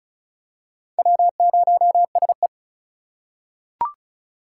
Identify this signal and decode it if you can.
Morse code, heard as a keyed tone, about 36 wpm, 700 Hz. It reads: W0SE
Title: USA Call Signs - 4 Characters - 3x Character Spacing - 35wpm